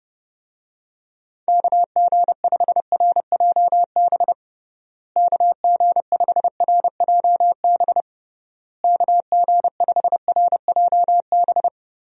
Morse code, heard as a keyed tone, about 30 wpm, 700 Hz. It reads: KG5RJB KG5RJB KG5RJB
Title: USA Call Signs - 6 Characters - Repeated 3x - 30wpm